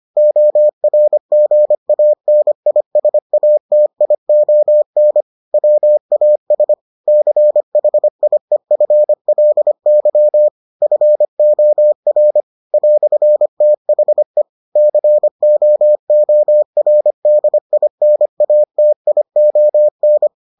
Morse code, heard as a keyed tone, about 25 wpm, 600 Hz. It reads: ORGANISATION WAS CHIEFLY FOR "THE COORDINATION